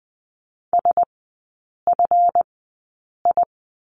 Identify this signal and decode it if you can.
Morse code, heard as a keyed tone, about 20 wpm, spaced as for 8 wpm, 700 Hz. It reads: SFI
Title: 145 Lesson 24e - Letter y - Three Characters 20wpm